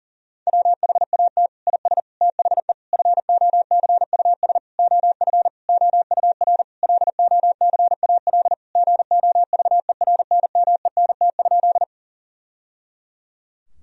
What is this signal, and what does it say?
Morse code, heard as a keyed tone, about 40 wpm, 700 Hz. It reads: WHAT IS THE FOCUS OF OUR LOCAL GOVERNMENT?
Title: Sentences from Top 1000 Words - Review Part 4 - 40wpm